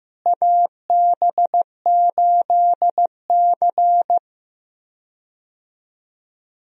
Morse code, heard as a keyed tone, about 15 wpm, 700 Hz. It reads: AB8C
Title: USA Call Signs - 4 Characters - Repeated 2x - 15wpm